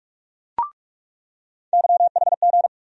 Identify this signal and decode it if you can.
Morse code, heard as a keyed tone, about 43 wpm, 700 Hz. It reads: YHG
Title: USA Call Sign Postfixes - 3 Letters 45wpm